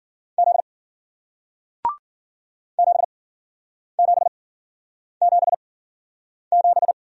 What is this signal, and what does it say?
Morse code, heard as a keyed tone, about 53 wpm, spaced as for 9 wpm, 700 Hz. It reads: Z 7777